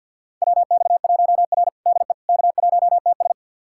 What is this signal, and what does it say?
Morse code, heard as a keyed tone, about 52 wpm, 700 Hz. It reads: WX1R DE K1TS
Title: USA Callsign DE Callsign - 4 Characters - 50wpm